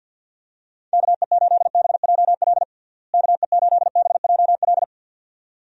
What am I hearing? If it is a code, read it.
Morse code, heard as a keyed tone, about 47 wpm, 700 Hz. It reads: KE8BJL KE8BJL
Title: USA Call Signs - 6 Characters - Repeated 2x - 50wpm